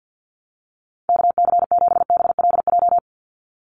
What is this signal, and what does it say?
Morse code, heard as a keyed tone, about 48 wpm, 700 Hz. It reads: KC7BLJ
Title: USA Call Signs 50wpm